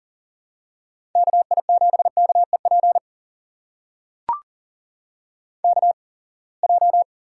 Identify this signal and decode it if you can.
Morse code, heard as a keyed tone, about 39 wpm, spaced as for 27 wpm, 700 Hz. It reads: KI7KEP KJ
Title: USA Call Signs - 6 Characters - 8x Character Spacing - 40wpm